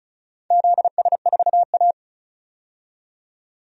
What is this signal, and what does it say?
Morse code, heard as a keyed tone, about 34 wpm, 700 Hz. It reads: ZS4A
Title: DXCC-100 Call Signs - 4 Characters - 35wpm